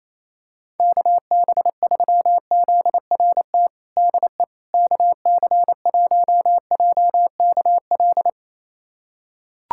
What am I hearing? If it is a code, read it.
Morse code, heard as a keyed tone, about 28 wpm, 700 Hz. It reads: KB3ZRT DE KC1JKL
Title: USA Callsign DE Callsign 28wpm